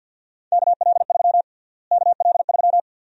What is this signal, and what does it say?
Morse code, heard as a keyed tone, about 51 wpm, 700 Hz. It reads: KL3 KL3
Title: USA Call Sign Prefixes - Repeated 2x - 50wpm